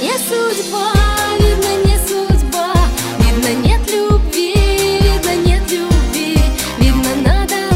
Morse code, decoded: none